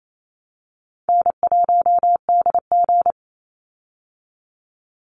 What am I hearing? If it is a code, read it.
Morse code, heard as a keyed tone, about 28 wpm, 690 Hz. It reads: N1DG
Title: Call Signs Repeated 3x - 28wpm